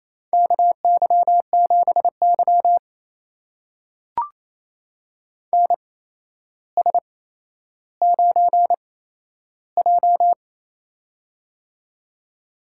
Morse code, heard as a keyed tone, about 28 wpm, spaced as for 12 wpm, 700 Hz. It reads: KY7Y NS9J
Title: USA Call Signs - 4 Characters - 8x Character Spacing - 28wpm